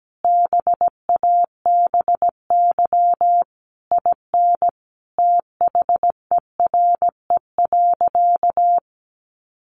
Morse code, heard as a keyed tone, about 17 wpm, 700 Hz. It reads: BABY IN THERE.
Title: Easy Phrases 17wpm